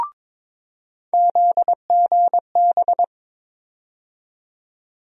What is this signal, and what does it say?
Morse code, heard as a keyed tone, about 22 wpm, 700 Hz. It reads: ZGB
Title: USA Call Sign Postfixes - 3 Letters 22wpm